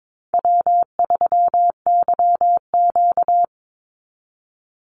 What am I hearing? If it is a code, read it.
Morse code, heard as a keyed tone, about 22 wpm, 700 Hz. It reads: W3YQ
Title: Easy Call Signs Repeated 2x - 22wpm